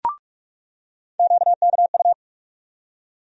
Morse code, heard as a keyed tone, about 46 wpm, 700 Hz. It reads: QKU
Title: USA Call Sign Postfixes - 3 Letters 45wpm